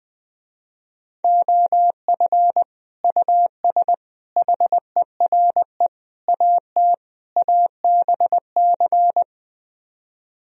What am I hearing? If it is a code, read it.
Morse code, heard as a keyed tone, about 20 wpm, 700 Hz. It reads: OF US HERE AT ABC
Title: Speed-Racing - Sets of 5 Words 20wpm